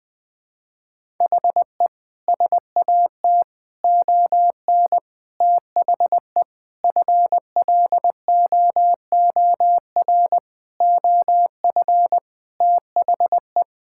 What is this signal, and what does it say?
Morse code, heard as a keyed tone, about 20 wpm, 700 Hz. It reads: HE SAT ON THE FLOOR OF THE
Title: Sentences with 5-Word Phrases - 20wpm